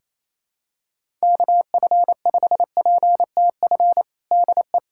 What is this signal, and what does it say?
Morse code, heard as a keyed tone, about 28 wpm, 700 Hz. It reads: KF5PTF DE
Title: USA Callsign DE Callsign - 6 Characters - 28wpm